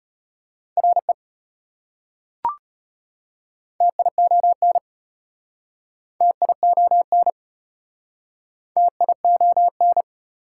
Morse code, heard as a keyed tone, about 33 wpm, 700 Hz. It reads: RE TION TION TION